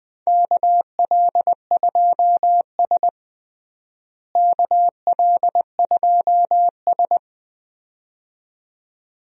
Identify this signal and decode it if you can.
Morse code, heard as a keyed tone, about 21 wpm, 700 Hz. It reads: KL2S KL2S
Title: USA Call Signs - 4 Characters - Repeated 2x - 20wpm